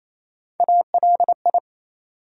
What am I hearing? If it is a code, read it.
Morse code, heard as a keyed tone, about 28 wpm, 700 Hz. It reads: ALI
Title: USA Call Sign Postfixes - 3 Letters 28wpm